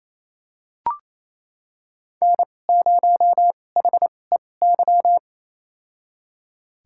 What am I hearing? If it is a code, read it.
Morse code, heard as a keyed tone, about 28 wpm, spaced as for 20 wpm, 700 Hz. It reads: N0HEY
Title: USA Call Signs - 5 Characters - 2x Character Spacing - 28wpm